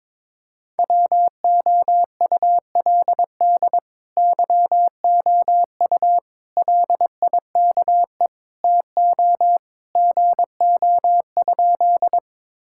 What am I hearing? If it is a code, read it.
Morse code, heard as a keyed tone, about 22 wpm, 700 Hz. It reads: WOULD YOU LIKE TO GO?
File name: practice / 22wpm / Sentences with 2-Word Phrases - 22wpm